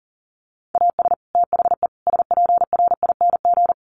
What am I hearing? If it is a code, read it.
Morse code, heard as a keyed tone, about 40 wpm, 700 Hz. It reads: AS THE SPRING
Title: Sentences from Top 1200 Words 40wpm